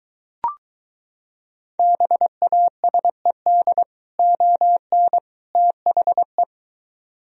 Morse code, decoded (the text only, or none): BASED ON THE